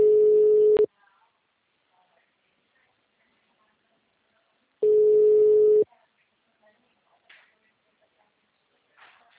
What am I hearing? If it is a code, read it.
Ringback tone (Europe, 425 Hz), with speech.